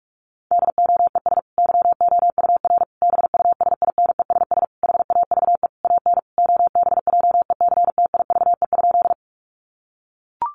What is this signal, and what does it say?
Morse code, heard as a keyed tone, about 44 wpm, 700 Hz. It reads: DOES YOUR BUSINESS HAVE AN OBJECTIVE?
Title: Sentences from Top 1200 Words 45wpm